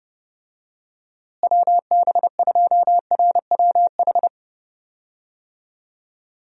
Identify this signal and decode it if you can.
Morse code, heard as a keyed tone, about 30 wpm, 700 Hz. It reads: WB2RWH
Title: USA Call Signs - 6 Characters - Repeated 3x - 30wpm